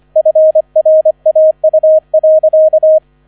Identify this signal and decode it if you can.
Morse code, heard as a keyed tone, about 25 wpm, 630 Hz. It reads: FRAU.